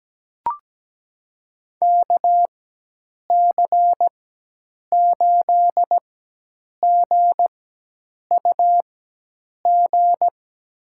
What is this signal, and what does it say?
Morse code, heard as a keyed tone, about 17 wpm, spaced as for 8 wpm, 700 Hz. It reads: KC8GUG
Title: USA Call Signs - 4x Character Spacing - 17wpm